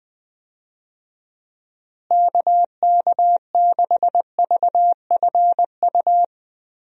Morse code, heard as a keyed tone, about 19 wpm, 700 Hz. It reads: KK6VFU
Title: USA Call Signs - 6 Characters - Repeated 3x - 20wpm